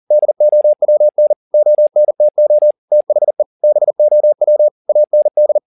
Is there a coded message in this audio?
Yes: Morse code from 0.1 to 5.6 s, heard as a keyed tone, about 40 wpm, 600 Hz. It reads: DOWN ONTO THE BOW AND